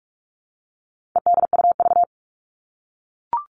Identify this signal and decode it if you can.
Morse code, heard as a keyed tone, about 44 wpm, 700 Hz. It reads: EDUV
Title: All Cut Numbers - 4-Digit 45wpm